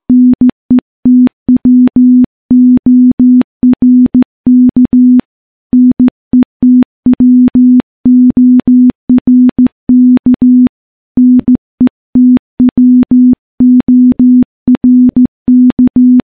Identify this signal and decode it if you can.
Morse code, heard as a keyed tone, about 15 wpm, 250 Hz. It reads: NETWORK NETWORK NETWORK
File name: morse / morse-training-15WPM